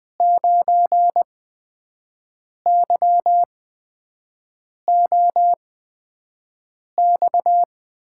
Morse code, heard as a keyed tone, about 20 wpm, spaced as for 5 wpm, 700 Hz. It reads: 9YOX
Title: USA Call Signs - 5 Characters - 8x Character Spacing - 20wpm